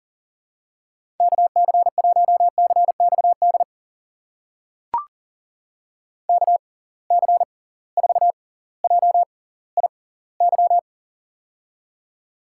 Morse code, heard as a keyed tone, about 39 wpm, spaced as for 20 wpm, 700 Hz. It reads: KC1CXD KC4JIY
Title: USA Call Signs - 6 Characters - 6x Character Spacing - 40wpm